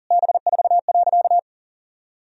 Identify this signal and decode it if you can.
Morse code, heard as a keyed tone, about 40 wpm, 700 Hz. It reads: B4.
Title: Sentences from Top 400 Words and Abbreviations 40wpm